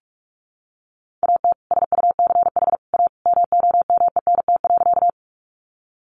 Morse code, heard as a keyed tone, about 44 wpm, 700 Hz. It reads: AT SUCH A MOMENT.
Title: Sentences from Top 1000 Words 45wpm